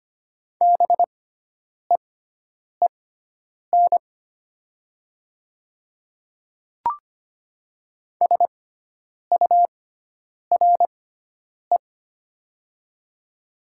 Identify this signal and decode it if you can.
Morse code, heard as a keyed tone, about 25 wpm, spaced as for 9 wpm, 700 Hz. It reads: BEEN SURE
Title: 6x Character Spacing - Top 300 Words - 25wpm